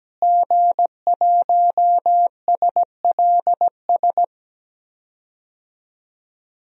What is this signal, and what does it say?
Morse code, heard as a keyed tone, about 17 wpm, 700 Hz. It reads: G1SLS
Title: DXCC-100 Call Signs - 5 Characters - Repeated 2x - 17wpm